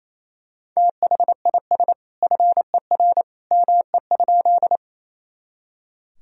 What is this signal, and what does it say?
Morse code, heard as a keyed tone, about 27 wpm, 700 Hz. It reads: THIS FER ME?
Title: Sentences from Top 200 Words and Abbreviations 28wpm